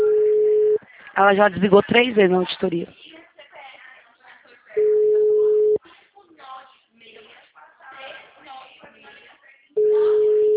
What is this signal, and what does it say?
Ringback tone (Europe, 425 Hz), with speech.
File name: out- -2010- - - .3494